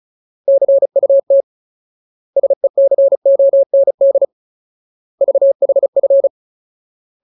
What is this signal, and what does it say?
Morse code, heard as a keyed tone, about 34 wpm, 550 Hz. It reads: CUT SECOND VHF
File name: Ultimate Word List / 35WPM Ultimate